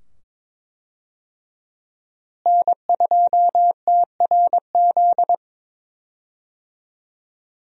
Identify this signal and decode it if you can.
Morse code, heard as a keyed tone, about 22 wpm, 700 Hz. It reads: N2TRZ
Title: USA Call Signs - 5 Characters - Repeated 2x - 22wpm